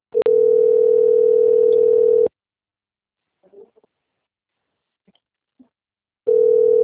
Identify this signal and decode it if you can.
Ringback tone (North America), 440 + 480 Hz.
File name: rg-1600-unknown- - - .42914